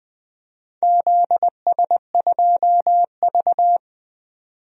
Morse code, heard as a keyed tone, about 20 wpm, 700 Hz. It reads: ZS2V
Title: DXCC-100 Call Signs - 4 Characters - 20wpm